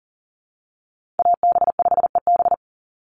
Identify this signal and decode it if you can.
Morse code, heard as a keyed tone, about 40 wpm, 700 Hz. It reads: AB5EB
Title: Call Signs Repeated 2x - 40wpm